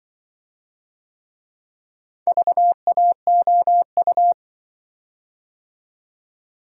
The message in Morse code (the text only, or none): VAOU